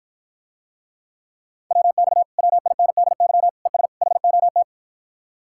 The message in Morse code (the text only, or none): WX WINDY ES HOT